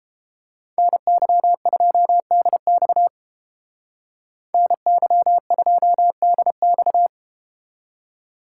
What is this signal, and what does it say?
Morse code, heard as a keyed tone, about 31 wpm, 700 Hz. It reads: NY2DX NY2DX